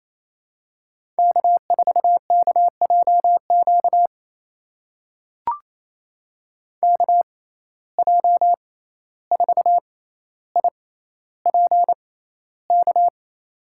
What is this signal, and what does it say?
Morse code, heard as a keyed tone, about 28 wpm, spaced as for 14 wpm, 700 Hz. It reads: K4KJQ KJ4IPK
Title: USA Call Signs - 6x Character Spacing - 28wpm